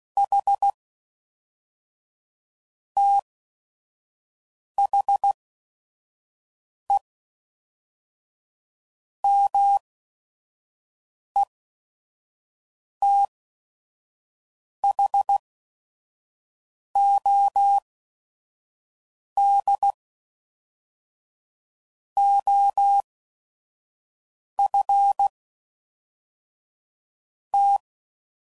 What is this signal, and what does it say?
Morse code, heard as a keyed tone, about 16 wpm, spaced as for 5 wpm, 800 Hz. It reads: H THE METHOD OF T
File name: CW Code Groups @5WPM / Technical Subjects / AND Gate